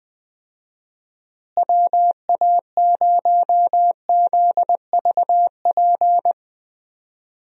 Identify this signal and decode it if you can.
Morse code, heard as a keyed tone, about 21 wpm, 700 Hz. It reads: WA0ZVP
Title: USA Call Signs - 6 Characters - Repeated 2x - 20wpm